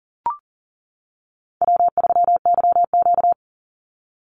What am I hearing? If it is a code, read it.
Morse code, heard as a keyed tone, about 40 wpm, 700 Hz. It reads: W3YQ
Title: Call Signs - 40wpm